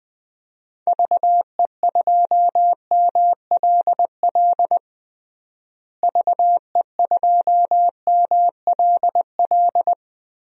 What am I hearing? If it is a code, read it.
Morse code, heard as a keyed tone, about 20 wpm, 700 Hz. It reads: VE2MLL VE2MLL
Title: DXCC-100 Call Signs - 6 Characters - Repeated 2x - 20wpm